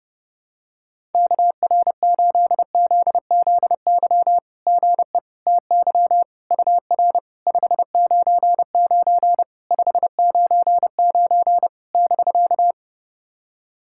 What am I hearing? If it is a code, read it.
Morse code, heard as a keyed tone, about 30 wpm, 700 Hz. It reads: KR8ZZY GE TY UR 599 599 <BK>
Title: SOTA QSO Practice 30wpm